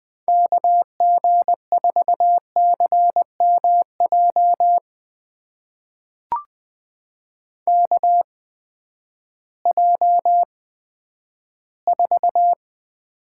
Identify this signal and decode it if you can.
Morse code, heard as a keyed tone, about 20 wpm, spaced as for 11 wpm, 700 Hz. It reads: KG4CMJ KJ4
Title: USA Call Signs - 6 Characters - 8x Character Spacing - 20wpm